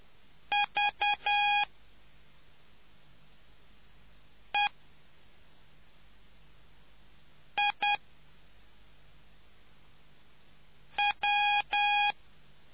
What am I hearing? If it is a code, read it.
Morse code, heard as a keyed tone, about 10 wpm, 820 Hz. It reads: V E I W